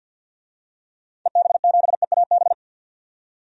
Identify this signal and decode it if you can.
Morse code, heard as a keyed tone, about 53 wpm, 700 Hz. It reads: EB7EAB
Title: DXCC-100 Call Signs - 6 Characters - 50wpm